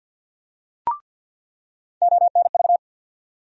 Morse code, heard as a keyed tone, about 49 wpm, 700 Hz. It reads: ONV
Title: USA Call Sign Postfixes - 3 Letters 50wpm